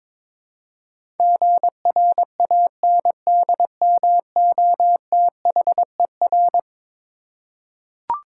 Morse code, heard as a keyed tone, about 22 wpm, 700 Hz. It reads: GRANDMOTHER
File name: practice / 22wpm / Eleven Letter Words - 22wpm